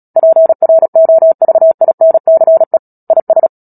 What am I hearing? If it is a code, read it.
Morse code, heard as a keyed tone, about 37 wpm, 650 Hz. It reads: PROVINCE IS
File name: Book Two / 10-Chapter 10 - Epilogue - 36 WPM